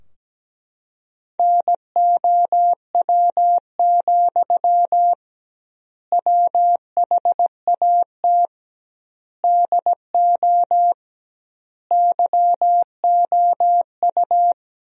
Morse code, heard as a keyed tone, about 17 wpm, 700 Hz. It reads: NOW, WHAT DO YOU
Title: 2x Word-Spacing - Sentences from Top 200 Words - 17wpm